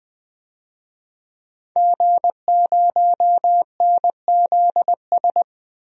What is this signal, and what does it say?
Morse code, heard as a keyed tone, about 20 wpm, 700 Hz. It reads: G0NZS
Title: DXCC-100 Call Signs - 5 Characters - 20wpm